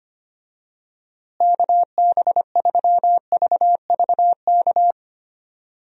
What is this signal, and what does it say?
Morse code, heard as a keyed tone, about 25 wpm, 700 Hz. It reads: KB3VVK
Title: USA Call Signs - 6 Characters - Repeated 3x - 25wpm